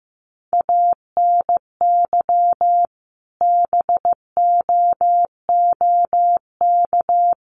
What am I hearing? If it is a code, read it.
Morse code, heard as a keyed tone, about 15 wpm, 700 Hz. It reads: ANY BOOK
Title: Sentences from Top 300 Words 15wpm